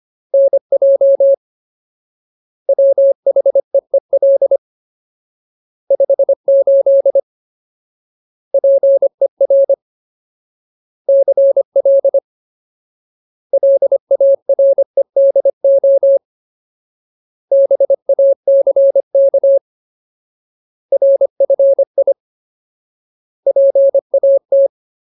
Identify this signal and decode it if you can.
Morse code, heard as a keyed tone, about 25 wpm, 550 Hz. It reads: NJ WHEEL 58 PER CL LAREDO BACK RFI PAT
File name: Ultimate Word List / 25WPM Ultimate